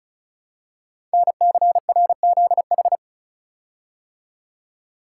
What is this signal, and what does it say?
Morse code, heard as a keyed tone, about 36 wpm, 700 Hz. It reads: NCRZH
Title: Code Groups - 5 Letters - 35wpm